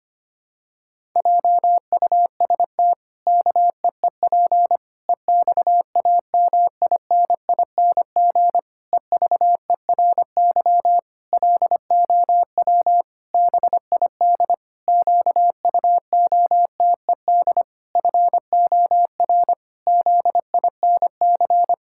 Morse code, heard as a keyed tone, about 25 wpm, 700 Hz. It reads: JUST KEEP EXAMINING EVERY LOW BID QUOTED FOR ZINC
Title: Pangram Challenge v2 - 25wpm